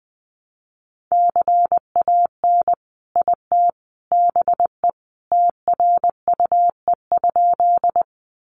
Morse code, heard as a keyed tone, about 20 wpm, 700 Hz. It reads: CAN IT BE TRUE?
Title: Sentences from Top 400 Words 20wpm